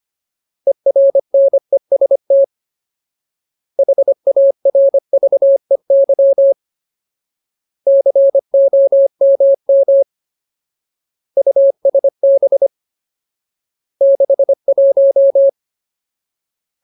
Morse code, heard as a keyed tone, about 25 wpm, 550 Hz. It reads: ERNEST HARVEY COMM USB 61
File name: Ultimate Word List / 25WPM Ultimate